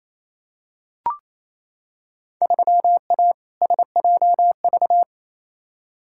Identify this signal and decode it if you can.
Morse code, heard as a keyed tone, about 28 wpm, 700 Hz. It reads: 3A SJV
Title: Field Day Exchanges Encore - 28wpm